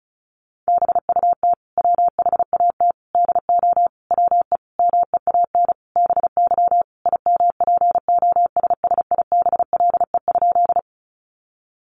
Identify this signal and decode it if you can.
Morse code, heard as a keyed tone, about 34 wpm, 700 Hz. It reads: BUT WHAT DO WE MEAN BY IMPOSSIBLE?
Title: Sentences from Top 1100 Words 35wpm